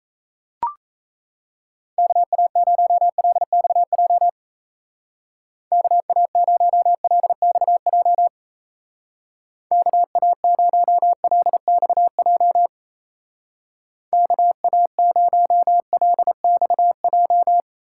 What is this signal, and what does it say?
Morse code, heard as a keyed tone, about 35 wpm, 700 Hz. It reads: KA0LXJ KA0LXJ KA0LXJ KA0LXJ